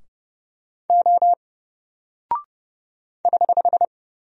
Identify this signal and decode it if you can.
Morse code, heard as a keyed tone, about 30 wpm, 700 Hz. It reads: O <HH>
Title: Single Character (Letter:Number:Punctuation:Prosign) 30wpm